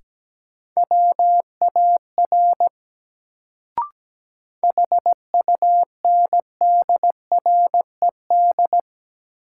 Morse code, heard as a keyed tone, about 17 wpm, 700 Hz. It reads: WAR HUNDRED
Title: Sentences from Top 400 Words - New Word Drill - 17wpm